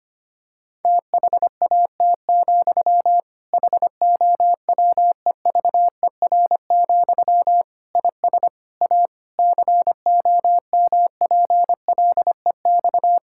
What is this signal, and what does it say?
Morse code, heard as a keyed tone, about 25 wpm, 700 Hz. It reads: THAT, HOWEVER, IS A COMPLEX
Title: Sentences from Top 1000 Words - Review Part 6 - 25wpm